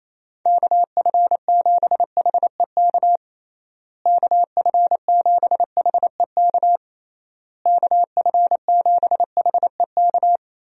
Morse code, heard as a keyed tone, about 28 wpm, 700 Hz. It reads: KF7HEK KF7HEK KF7HEK
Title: USA Call Signs - 6 Characters - Repeated 3x - 28wpm